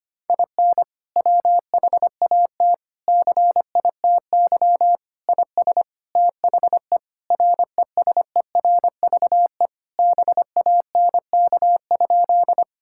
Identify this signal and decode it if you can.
Morse code, heard as a keyed tone, about 25 wpm, 700 Hz. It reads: IN WHAT CITY IS THE RESERVE BANK?
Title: Sentences from Top 1000 Words - Review Part 6 - 25wpm